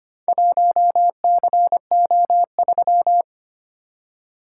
Morse code, heard as a keyed tone, about 25 wpm, 700 Hz. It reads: 1CO3
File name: practice / 25wpm / Four Letter-Number Combinations - 25wpm